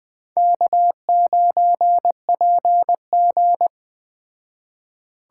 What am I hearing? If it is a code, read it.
Morse code, heard as a keyed tone, about 20 wpm, 700 Hz. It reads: K9PG